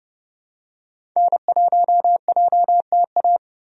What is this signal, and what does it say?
Morse code, heard as a keyed tone, about 30 wpm, 700 Hz. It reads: N1JTA
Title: USA Call Signs - 5 Characters - 30wpm